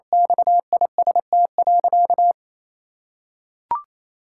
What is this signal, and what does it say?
Morse code, heard as a keyed tone, about 28 wpm, 700 Hz. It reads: XIST.
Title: Sentences from Top 1000 Words - Review Part 5 - 28wpm